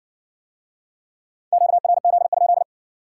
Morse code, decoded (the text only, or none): QRZ?